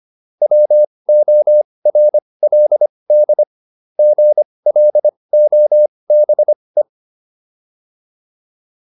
Morse code, heard as a keyed tone, about 25 wpm, spaced as for 20 wpm, 600 Hz. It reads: WORLD GLOBE